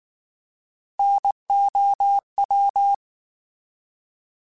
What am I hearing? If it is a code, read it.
Morse code, heard as a keyed tone, about 19 wpm, 800 Hz. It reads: NOW
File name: Alphabetical / Alphabetical mp3s / Ns @18